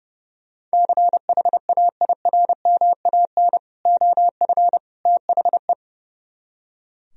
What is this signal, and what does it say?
Morse code, heard as a keyed tone, about 35 wpm, 700 Hz. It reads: CHAIRMAN OF THE
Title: Speed-Racing - Sets of 3 Words 30wpm